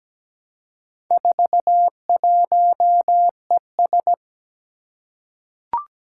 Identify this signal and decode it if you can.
Morse code, heard as a keyed tone, about 17 wpm, 700 Hz. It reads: 41ES
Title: Four Challenging Letter-Number Combinations 17wpm